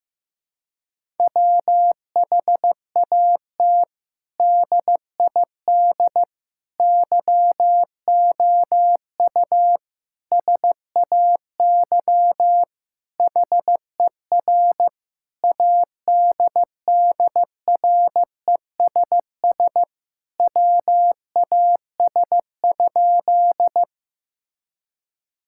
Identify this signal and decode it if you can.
Morse code, heard as a keyed tone, about 15 wpm, 700 Hz. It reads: WHAT DID YOU SAY HER ADDRESS WAS?
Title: Sentences from Top 1000 Words - Review Part 6 - 15wpm